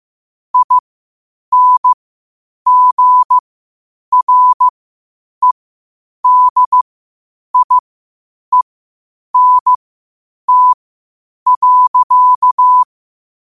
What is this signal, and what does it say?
Morse code, heard as a keyed tone, about 15 wpm, spaced as for 8 wpm, 1000 Hz. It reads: INGREDIENT.